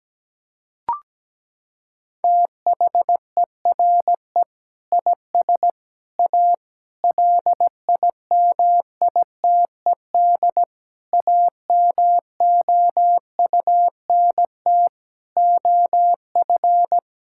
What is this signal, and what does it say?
Morse code, heard as a keyed tone, about 17 wpm, 700 Hz. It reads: THERE IS A LIMITED AMOUNT OF
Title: Sentences from Top 1000 Words - Review Part 6 - 17wpm